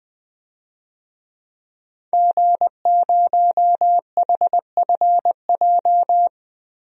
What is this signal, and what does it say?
Morse code, heard as a keyed tone, about 20 wpm, 700 Hz. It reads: G0HFJ
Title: DXCC-100 Call Signs - 5 Characters - 20wpm